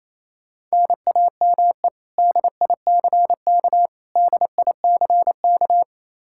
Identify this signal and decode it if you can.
Morse code, heard as a keyed tone, about 27 wpm, 700 Hz. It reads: NAME DICK DICK
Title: Practice QSO - Exchange 1 - Beginning - 28wpm